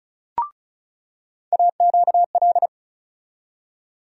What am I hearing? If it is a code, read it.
Morse code, heard as a keyed tone, about 36 wpm, 700 Hz. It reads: AQL
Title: USA Call Sign Postfixes - 3 Letters 35wpm